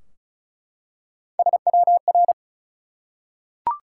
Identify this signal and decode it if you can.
Morse code, heard as a keyed tone, about 36 wpm, 700 Hz. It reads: SWR
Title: Top 180 QSO Elements 35wpm